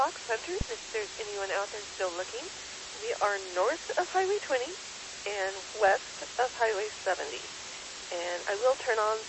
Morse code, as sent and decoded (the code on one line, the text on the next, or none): none